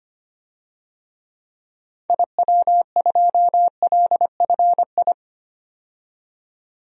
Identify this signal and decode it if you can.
Morse code, heard as a keyed tone, about 25 wpm, 700 Hz. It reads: IW2LFI
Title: DXCC-100 Call Signs - 6 Characters - 25wpm